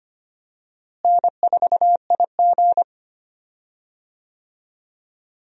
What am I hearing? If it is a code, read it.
Morse code, heard as a keyed tone, about 25 wpm, 700 Hz. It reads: N4IG
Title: USA Call Signs - 4 Characters - Repeated 2x - 25wpm